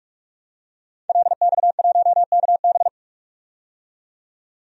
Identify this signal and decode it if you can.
Morse code, heard as a keyed tone, about 46 wpm, 700 Hz. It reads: LX1KB
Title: DXCC-100 Call Signs - 5 Characters - 45wpm